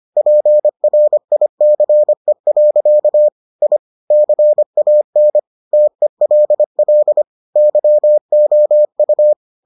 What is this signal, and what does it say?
Morse code, heard as a keyed tone, about 25 wpm, 600 Hz. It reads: PRICE. I CAN TELL YOU